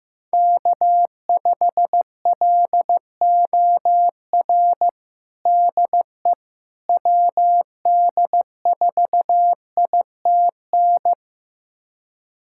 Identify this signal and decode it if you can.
Morse code, heard as a keyed tone, about 15 wpm, 700 Hz. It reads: K5LOR DE WD4ITN
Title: USA Callsign DE Callsign 15wpm